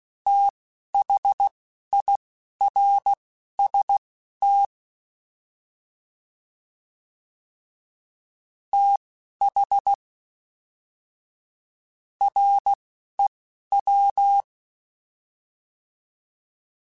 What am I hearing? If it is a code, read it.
Morse code, heard as a keyed tone, about 16 wpm, spaced as for 9 wpm, 800 Hz. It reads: THIRST THREW
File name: TH / TH mp3s / Fast TH @10